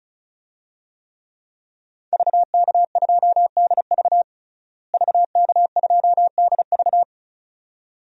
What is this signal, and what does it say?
Morse code, heard as a keyed tone, about 36 wpm, 700 Hz. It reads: VK2DV VK2DV
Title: DXCC-100 Call Signs - 5 Characters - Repeated 2x - 35wpm